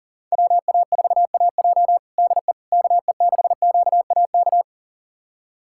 Morse code, heard as a keyed tone, about 37 wpm, 700 Hz. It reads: WA4AJ DE KE6QAK
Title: USA Callsign DE Callsign 40wpm